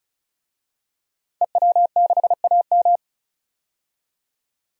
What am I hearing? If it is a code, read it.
Morse code, heard as a keyed tone, about 34 wpm, 700 Hz. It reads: EW6AM
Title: DXCC-100 Call Signs - 5 Characters - 35wpm